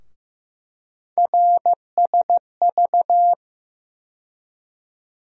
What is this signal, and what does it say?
Morse code, heard as a keyed tone, about 15 wpm, 700 Hz. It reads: RSV
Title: USA Call Sign Postfixes - 3 Letters 15wpm